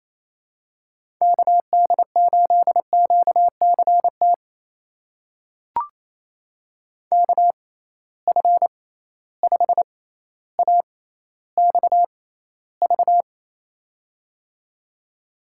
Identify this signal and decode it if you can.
Morse code, heard as a keyed tone, about 28 wpm, spaced as for 14 wpm, 700 Hz. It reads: KD8QCT KF5AXV
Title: USA Call Signs - 6 Characters - 6x Character Spacing - 28wpm